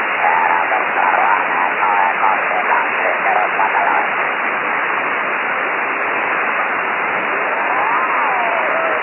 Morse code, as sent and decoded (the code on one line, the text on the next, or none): none